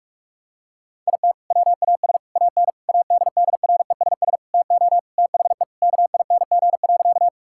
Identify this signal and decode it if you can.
Morse code, heard as a keyed tone, about 46 wpm, 700 Hz. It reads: IT WAS AN ADDRESS TO THE KING.
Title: Sentences from Top 1000 Words - Review Part 6 - 45wpm